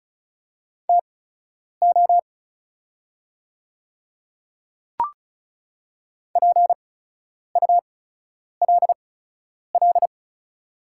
Morse code, heard as a keyed tone, about 34 wpm, spaced as for 10 wpm, 700 Hz. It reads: TO PULL
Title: 8x Character Spacing - Top 500 Words - 35wpm